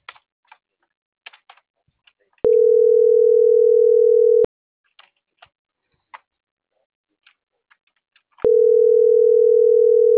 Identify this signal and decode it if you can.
Ringback tone (North America), 440 + 480 Hz.